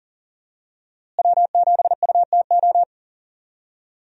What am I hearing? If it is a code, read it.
Morse code, heard as a keyed tone, about 37 wpm, 700 Hz. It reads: W7UTO
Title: USA Call Signs - 5 Characters - Repeated 3x - 40wpm